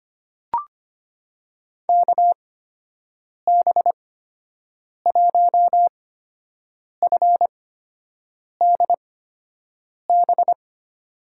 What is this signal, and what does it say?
Morse code, heard as a keyed tone, about 25 wpm, spaced as for 7 wpm, 700 Hz. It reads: KB1FDB
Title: USA Call Signs - 6 Characters - 8x Character Spacing - 25wpm